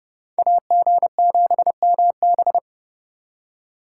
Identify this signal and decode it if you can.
Morse code, heard as a keyed tone, about 27 wpm, 700 Hz. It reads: AG7MB